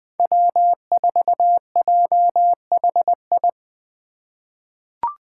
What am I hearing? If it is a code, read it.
Morse code, heard as a keyed tone, about 20 wpm, 700 Hz. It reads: W4JHI
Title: Five Challenging Letter-Number Combinations 20wpm